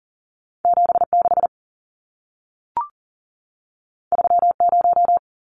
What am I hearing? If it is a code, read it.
Morse code, heard as a keyed tone, about 40 wpm, 700 Hz. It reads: 76 30
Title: Numbers - 2-Digit 40wpm